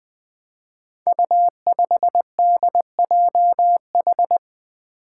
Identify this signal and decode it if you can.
Morse code, heard as a keyed tone, about 20 wpm, 700 Hz. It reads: U5DJH
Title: Five Challenging Letter-Number Combinations 20wpm